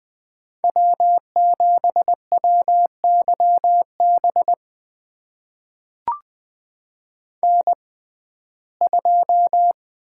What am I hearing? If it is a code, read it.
Morse code, heard as a keyed tone, about 20 wpm, spaced as for 15 wpm, 700 Hz. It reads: W7WYB N2
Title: USA Call Signs - 5 Characters - 6x Character Spacing - 20wpm